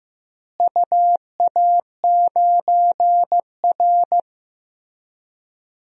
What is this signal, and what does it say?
Morse code, heard as a keyed tone, about 15 wpm, 700 Hz. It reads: UA9R
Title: DXCC-100 Call Signs - 4 Characters - Repeated 3x - 15wpm